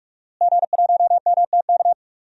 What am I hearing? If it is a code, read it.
Morse code, heard as a keyed tone, about 46 wpm, 700 Hz. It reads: G1MTK
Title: DXCC-100 Call Signs - 5 Characters - Repeated 2x - 45wpm